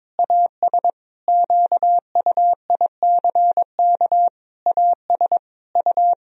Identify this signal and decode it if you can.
Morse code, heard as a keyed tone, about 22 wpm, 700 Hz. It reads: AS QUICK AS U
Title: Sentences from Top 500 Words and Abbreviations 22wpm